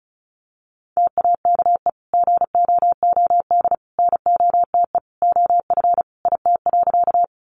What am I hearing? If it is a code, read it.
Morse code, heard as a keyed tone, about 34 wpm, 700 Hz. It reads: TAKE GOOD NOTE OF IT.
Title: Sentences from Top 500 Words 35wpm